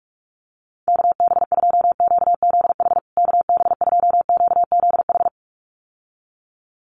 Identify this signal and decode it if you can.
Morse code, heard as a keyed tone, about 44 wpm, 700 Hz. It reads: KB2QZH KB2QZH
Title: USA Call Signs Repeated 2x 45wpm